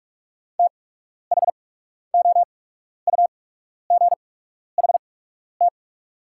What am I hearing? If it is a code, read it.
Morse code, heard as a keyed tone, about 47 wpm, spaced as for 12 wpm, 690 Hz. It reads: THOUGHT
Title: 8x Character Spacing - Top 300 Words - 45wpm